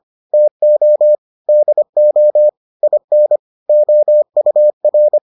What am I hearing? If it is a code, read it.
Morse code, heard as a keyed tone, about 25 wpm, 600 Hz. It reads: TO DO IN OUR